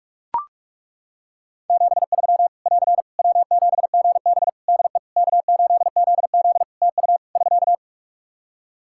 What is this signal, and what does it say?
Morse code, heard as a keyed tone, about 41 wpm, 700 Hz. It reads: 73 <AR> W7GB DE K8ZZ TU <SK>
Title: Practice QSO - Exchange 4 - Ending - 45wpm